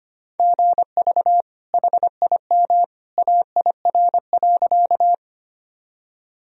Morse code, heard as a keyed tone, about 25 wpm, 700 Hz. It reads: GV HIM AIR.
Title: Sentences from Top 200 Words and Abbreviations 25wpm